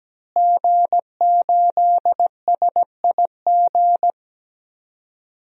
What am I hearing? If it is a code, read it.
Morse code, heard as a keyed tone, about 17 wpm, 700 Hz. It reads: G8SIG
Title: DXCC-100 Call Signs - 5 Characters - Repeated 3x - 17wpm